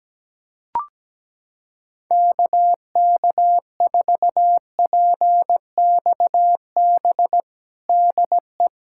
Morse code, heard as a keyed tone, about 17 wpm, 700 Hz. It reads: KK4PXB DE
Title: USA Callsign DE Callsign - 6 Characters - 17wpm